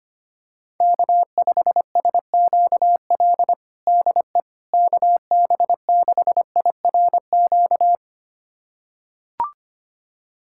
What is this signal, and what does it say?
Morse code, heard as a keyed tone, about 25 wpm, 700 Hz. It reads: K5SQL DE KB6IRQ
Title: USA Callsign DE Callsign 25wpm